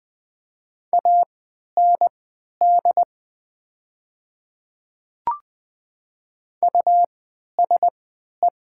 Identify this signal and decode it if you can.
Morse code, heard as a keyed tone, about 20 wpm, spaced as for 11 wpm, 700 Hz. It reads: AND USE
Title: 3x Character Spacing - Top 100 Words - 20wpm